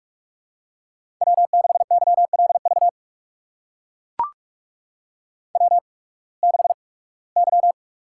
Morse code, heard as a keyed tone, about 44 wpm, spaced as for 23 wpm, 690 Hz. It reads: W6YLV W6Y
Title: USA Call Signs - 5 Characters - 8x Character Spacing - 45wpm